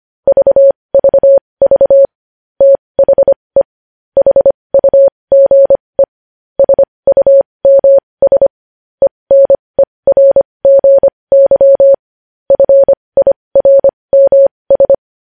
Morse code, heard as a keyed tone, about 25 wpm, spaced as for 20 wpm, 550 Hz. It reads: VVV THE HUGE SUMS ENERGY FIRMS
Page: News Headlines in Morse Code at 20 WPM